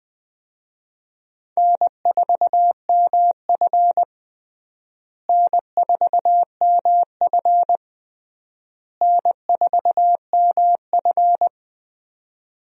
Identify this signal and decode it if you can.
Morse code, heard as a keyed tone, about 20 wpm, 700 Hz. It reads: N4MF N4MF N4MF
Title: USA Call Signs - 4 Characters - Repeated 3x - 20wpm